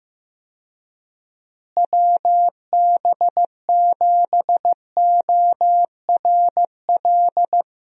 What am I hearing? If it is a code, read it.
Morse code, heard as a keyed tone, about 15 wpm, 700 Hz. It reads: WB7ORL
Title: USA Call Signs - 6 Characters - Repeated 3x - 15wpm